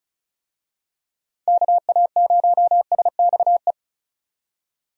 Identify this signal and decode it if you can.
Morse code, heard as a keyed tone, about 34 wpm, 700 Hz. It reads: KA0SXE
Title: USA Call Signs - 6 Characters - 35wpm